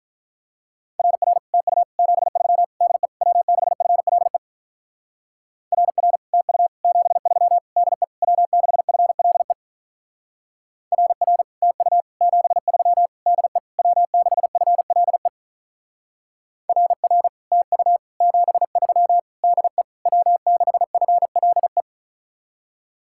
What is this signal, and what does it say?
Morse code, heard as a keyed tone, about 43 wpm, 700 Hz. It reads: RR TU 73 DE W6FLE RR TU 73 DE W6FLE RR TU 73 DE W6FLE RR TU 73 DE W6FLE
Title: Speed-Racing Common QSO Phrases Encore - 35wpm